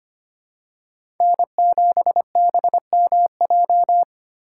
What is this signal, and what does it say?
Morse code, heard as a keyed tone, about 25 wpm, 700 Hz. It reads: N7BMJ
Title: USA Call Signs - 5 Characters - 8x Character Spacing - 25wpm